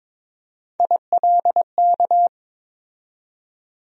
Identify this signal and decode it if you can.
Morse code, heard as a keyed tone, about 22 wpm, 700 Hz. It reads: ILK
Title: USA Call Sign Postfixes - 3 Letters 22wpm